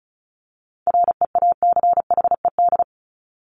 Morse code, heard as a keyed tone, about 34 wpm, 700 Hz. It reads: REACHED
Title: ED Suffix - Top 100 Words with Base Words 35wpm